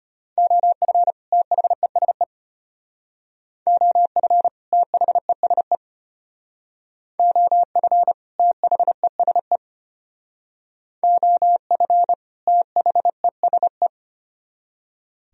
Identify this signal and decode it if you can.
Morse code, heard as a keyed tone, about 32 wpm, 700 Hz. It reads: OF THESE OF THESE OF THESE OF THESE